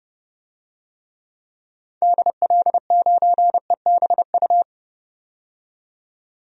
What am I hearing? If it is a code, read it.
Morse code, heard as a keyed tone, about 30 wpm, 700 Hz. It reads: DL9EBU
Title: DXCC-100 Call Signs - 6 Characters - 30wpm